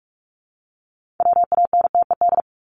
Morse code, heard as a keyed tone, about 44 wpm, 700 Hz. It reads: WANTED
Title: ED Suffix - Top 100 Words with Base Words 45wpm